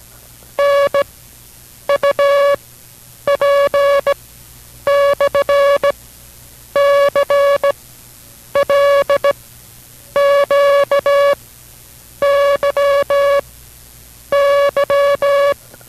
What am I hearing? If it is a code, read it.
Morse code, heard as a keyed tone, about 17 wpm, spaced as for 8 wpm, 560 Hz. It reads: NUP/CLQYY